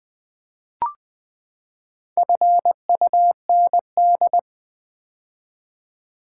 Morse code, heard as a keyed tone, about 20 wpm, 700 Hz. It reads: FUND
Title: Build A Word - 4 Letters At A Time - 11-Letter Words - 20wpm